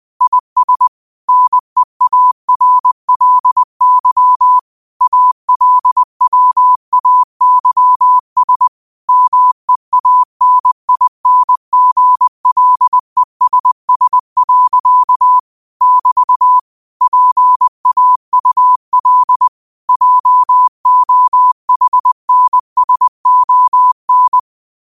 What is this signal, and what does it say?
Morse code, heard as a keyed tone, about 20 wpm, 1000 Hz. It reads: IS NEARLY ALWAYS MEANINGLESS. <BT> PAUL JOHNSON
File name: qotdfiles / 20wpm